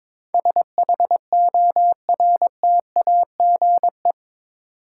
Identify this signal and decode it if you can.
Morse code, heard as a keyed tone, about 21 wpm, 700 Hz. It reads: SHORTAGE
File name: practice / 22wpm / Eight Letter Words - 22wpm